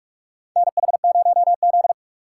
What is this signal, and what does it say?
Morse code, heard as a keyed tone, about 46 wpm, 700 Hz. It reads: NH0Z
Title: USA Call Signs - 4 Characters - 4x Character Spacing - 45wpm